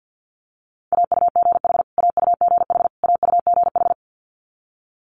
Morse code, heard as a keyed tone, about 50 wpm, 700 Hz. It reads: AUGH AUGH AUGH
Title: Multi-letter Phonemes Repeated 3x 50wpm